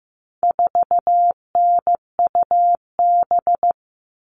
Morse code, heard as a keyed tone, about 15 wpm, 700 Hz. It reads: 4NUB
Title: All Cut Numbers - 4-Digit 15wpm